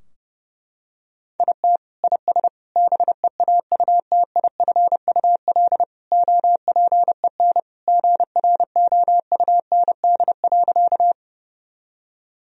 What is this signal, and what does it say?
Morse code, heard as a keyed tone, about 30 wpm, 700 Hz. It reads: IT IS BEAUTIFUL OPEN GROUND.
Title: Sentences from Top 1000 Words - Review Part 2 - 30wpm